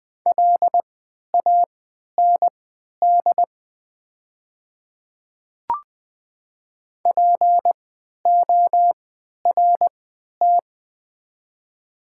Morse code, heard as a keyed tone, about 20 wpm, spaced as for 11 wpm, 700 Hz. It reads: LAND PORT
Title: 3x Character Spacing - Top 300 Words - 20wpm